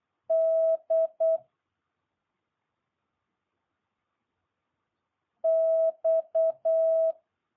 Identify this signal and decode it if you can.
Morse code, heard as a keyed tone, about 8 wpm, 650 Hz. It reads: D X
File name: jh8cbh